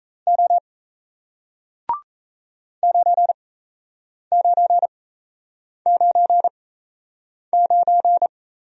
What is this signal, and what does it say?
Morse code, heard as a keyed tone, about 38 wpm, spaced as for 8 wpm, 700 Hz. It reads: O 9999